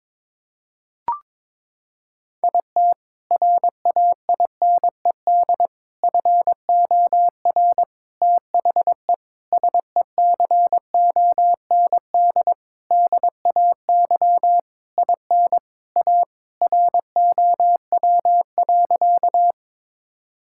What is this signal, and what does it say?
Morse code, heard as a keyed tone, about 22 wpm, 700 Hz. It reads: IT RAINED FOR THE SECOND DAY IN A ROW.
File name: Sentences with 7-Word Phrases - 22wpm